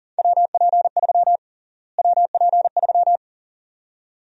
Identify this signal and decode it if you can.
Morse code, heard as a keyed tone, about 40 wpm, 700 Hz. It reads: WP3 WP3
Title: USA Call Sign Prefixes - Repeated 2x - 40wpm